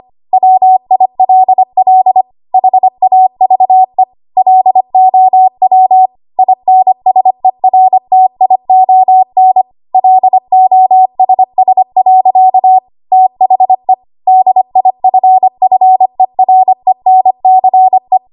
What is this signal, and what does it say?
Morse code, heard as a keyed tone, about 25 wpm, 750 Hz. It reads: WILL HAVE LOW INSERTION LOSS. THE DIFFERENCE
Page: W1AW Web Code Practice Files